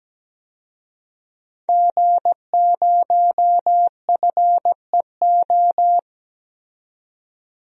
Morse code, heard as a keyed tone, about 17 wpm, 700 Hz. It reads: G0FEO
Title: DXCC-100 Call Signs - 5 Characters - 17wpm